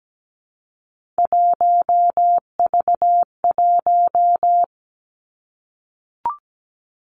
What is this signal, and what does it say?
Morse code, heard as a keyed tone, about 17 wpm, 700 Hz. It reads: 1V1
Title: Three Challenging Letter-Number Combinations 17wpm